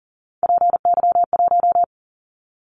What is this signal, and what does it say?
Morse code, heard as a keyed tone, about 40 wpm, 700 Hz. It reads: PY1
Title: Callsign Prefixes for Top 100 DXCC - 40wpm